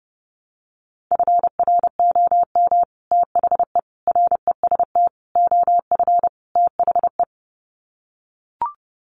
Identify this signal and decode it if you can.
Morse code, heard as a keyed tone, about 30 wpm, 700 Hz. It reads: FROM THE REST OF THE
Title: Sets of 5 Words 30wpm